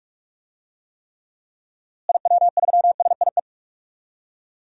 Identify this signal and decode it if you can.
Morse code, heard as a keyed tone, about 41 wpm, 700 Hz. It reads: IW3SIE
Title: DXCC-100 Call Signs - 6 Characters - Repeated 3x - 45wpm